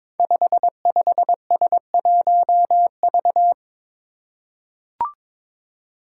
Morse code, heard as a keyed tone, about 22 wpm, 700 Hz. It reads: 55S1V
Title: Five Challenging Letter-Number Combinations 22wpm